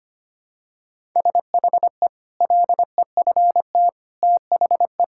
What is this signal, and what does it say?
Morse code, heard as a keyed tone, about 25 wpm, 700 Hz. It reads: SHE LEFT THE
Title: Sentences from Top 1000 Words - Review Part 5 - 25wpm